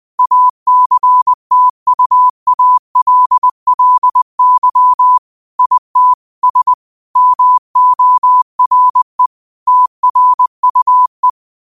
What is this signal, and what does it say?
Morse code, heard as a keyed tone, about 20 wpm, 1000 Hz. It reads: ACTUALLY IT S MORE TRUE